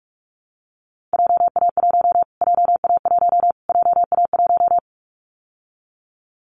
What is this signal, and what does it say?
Morse code, heard as a keyed tone, about 44 wpm, 700 Hz. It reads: JA1 JA1 JA1
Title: Callsign Prefixes for Top 25 DXCC - Repeated 3x - 45wpm